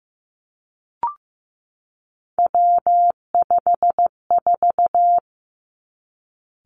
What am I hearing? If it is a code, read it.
Morse code, heard as a keyed tone, about 15 wpm, 700 Hz. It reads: W54
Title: Three Challenging Letter-Number Combinations 15wpm